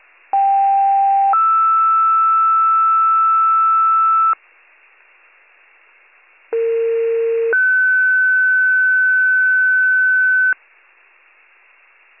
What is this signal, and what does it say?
Two-tone page, heard as a one-second tone, then a three-second tone, about 790 then 1320 Hz, about 460 then 1500 Hz. It is heard from 0.3 to 10.5 s.